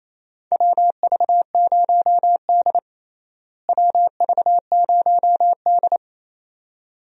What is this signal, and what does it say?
Morse code, heard as a keyed tone, about 28 wpm, 700 Hz. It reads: WV0D WV0D
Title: USA Call Signs - 4 Characters - Repeated 2x - 28wpm